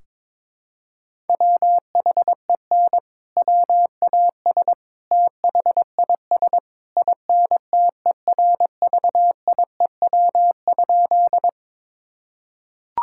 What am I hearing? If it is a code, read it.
Morse code, heard as a keyed tone, about 22 wpm, 700 Hz. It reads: WHEN WAS THIS INTERVIEW?
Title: Sentences from Top 1600 Words 22wpm